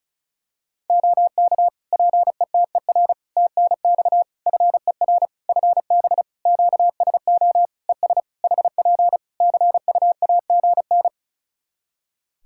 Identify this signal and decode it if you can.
Morse code, heard as a keyed tone, about 36 wpm, 700 Hz. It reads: OK PETER TNX FER FB QSO ES HP CUAGN
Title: K1USN SST QSO Practice 35wpm